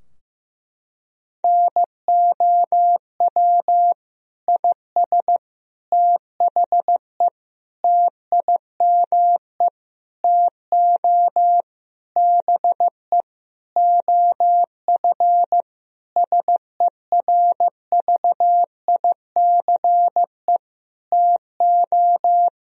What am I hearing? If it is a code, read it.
Morse code, heard as a keyed tone, about 16 wpm, 700 Hz. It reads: NOW IS THE TIME TO BE OF SERVICE TO
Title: Sentences from Top 1000 Words - Review Part 4 - 15wpm